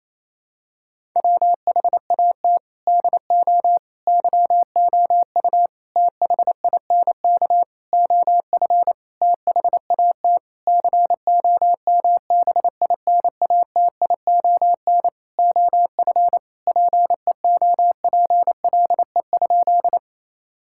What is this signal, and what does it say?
Morse code, heard as a keyed tone, about 28 wpm, 700 Hz. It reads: WHAT DO YOU THINK OF THAT COMBINATION OF PEOPLE?
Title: Sentences from Top 1000 Words - Review Part 6 - 28wpm